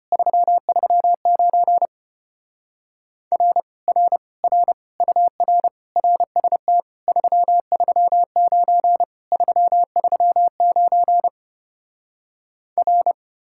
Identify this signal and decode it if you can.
Morse code, heard as a keyed tone, about 33 wpm, 700 Hz. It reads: 339 R R R UR RST 339 339 R